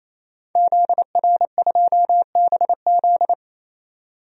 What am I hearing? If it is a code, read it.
Morse code, heard as a keyed tone, about 28 wpm, 700 Hz. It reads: ZR2BZ
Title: DXCC-100 Call Signs - 5 Characters - 28wpm